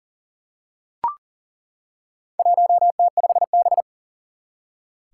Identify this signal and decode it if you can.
Morse code, heard as a keyed tone, about 37 wpm, 690 Hz. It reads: 1T5B
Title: Four Challenging Letter-Number Combinations 40wpm